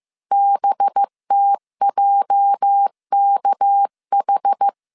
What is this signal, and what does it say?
Morse code, heard as a keyed tone, about 15 wpm, 800 Hz. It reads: BTJKH
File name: CW / DET / 1 9 1500C S B 14wpm 3302